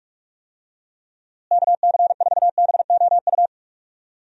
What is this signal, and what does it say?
Morse code, heard as a keyed tone, about 44 wpm, 700 Hz. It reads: KC4BOU
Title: USA Call Signs - 6 Characters - Repeated 2x - 45wpm